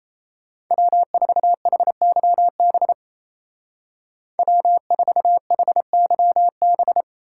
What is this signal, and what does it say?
Morse code, heard as a keyed tone, about 34 wpm, 700 Hz. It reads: W4HYB W4HYB
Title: Speed Racing - USA 1x3 Call Signs 28wpm